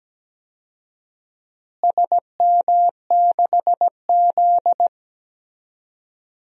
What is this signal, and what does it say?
Morse code, heard as a keyed tone, about 17 wpm, 700 Hz. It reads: SM6Z
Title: DXCC-100 Call Signs - 4 Characters - Repeated 2x - 17wpm